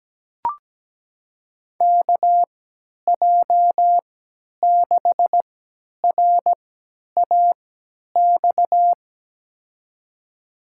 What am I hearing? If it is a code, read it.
Morse code, heard as a keyed tone, about 17 wpm, spaced as for 10 wpm, 700 Hz. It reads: KJ6RAX
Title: USA Call Signs - 6 Characters - 3x Character Spacing - 17wpm